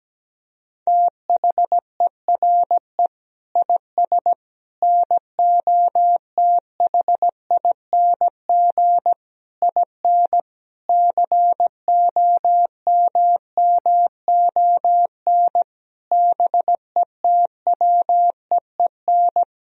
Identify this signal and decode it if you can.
Morse code, heard as a keyed tone, about 17 wpm, 700 Hz. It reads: THERE IS NOTHING IN COMMON BETWEEN
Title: Sentences from Top 1000 Words - Review Part 3 - 17wpm